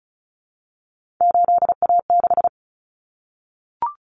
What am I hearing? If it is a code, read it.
Morse code, heard as a keyed tone, about 34 wpm, 700 Hz. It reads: 8A6
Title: Common Cut Numbers - 3-Digit 35wpm